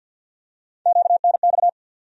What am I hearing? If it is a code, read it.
Morse code, heard as a keyed tone, about 50 wpm, 700 Hz. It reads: QNX
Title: USA Call Sign Postfixes - 3 Letters 50wpm